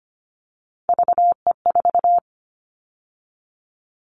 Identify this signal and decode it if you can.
Morse code, heard as a keyed tone, about 25 wpm, 690 Hz. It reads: VE4
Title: Callsign Prefixes for Top 50 DXCC - 25wpm